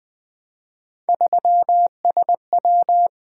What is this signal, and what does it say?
Morse code, heard as a keyed tone, about 20 wpm, 700 Hz. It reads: 3SW
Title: Three Letter-Number Combinations - 20wpm